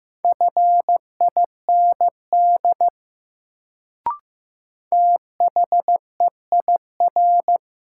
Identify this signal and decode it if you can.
Morse code, heard as a keyed tone, about 15 wpm, 700 Hz. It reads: FIND THEIR
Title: Sets of 2 Words - New Word Drill - 15wpm